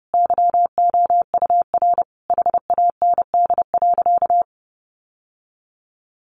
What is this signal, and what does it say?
Morse code, heard as a keyed tone, about 30 wpm, 700 Hz. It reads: YOUR HAND.
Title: Sentences from Top 1400 Words 30wpm